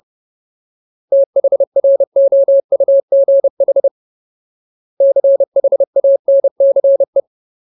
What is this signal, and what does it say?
Morse code, heard as a keyed tone, about 30 wpm, 550 Hz. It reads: THROUGH CHANCE
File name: Ultimate Word List / 30WPM Ultimate